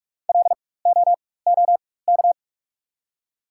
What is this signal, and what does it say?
Morse code, heard as a keyed tone, about 47 wpm, spaced as for 21 wpm, 700 Hz. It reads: LOOK
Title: 4x Character Spacing - Top 300 Words - 45wpm